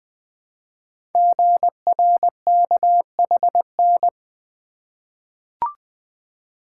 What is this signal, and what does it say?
Morse code, heard as a keyed tone, about 21 wpm, 700 Hz. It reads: GRKHN
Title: Code Groups - 5 Letter:Numbers - 20wpm